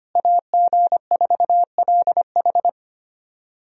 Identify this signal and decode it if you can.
Morse code, heard as a keyed tone, about 25 wpm, 700 Hz. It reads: AG4LH
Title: USA Call Signs - 5 Characters - Repeated 3x - 25wpm